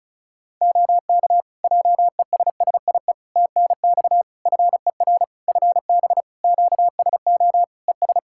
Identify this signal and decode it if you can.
Morse code, heard as a keyed tone, about 36 wpm, 700 Hz. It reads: OK JESSIE TNX FER FB QSO ES
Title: Practice QSO - Exchange 4 - Ending - 35wpm